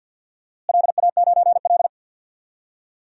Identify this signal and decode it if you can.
Morse code, heard as a keyed tone, about 52 wpm, 700 Hz. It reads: LA9L
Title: DXCC-100 Call Signs - 4 Characters - Repeated 2x - 50wpm